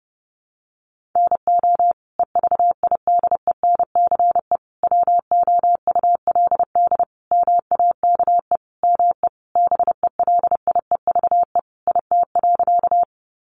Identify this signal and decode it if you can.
Morse code, heard as a keyed tone, about 30 wpm, 700 Hz. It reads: NO EVIDENCE WOULD MAKE ME BELIEVE IT.
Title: Sentences from Top 1300 Words 30wpm